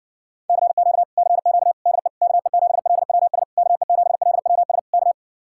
Morse code, heard as a keyed tone, about 70 wpm, 700 Hz. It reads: CQ CQ DE KE7LWS KE7LWS K